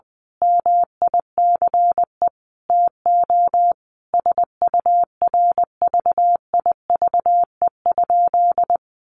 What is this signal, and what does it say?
Morse code, heard as a keyed tone, about 20 wpm, 700 Hz. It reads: MICE TO SURVIVE?
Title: A Sound Of Thunder 20wpm